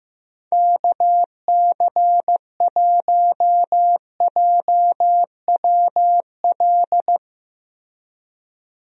Morse code, heard as a keyed tone, about 15 wpm, 700 Hz. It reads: KC1JWL
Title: USA Call Signs - 6 Characters - Repeated 2x - 15wpm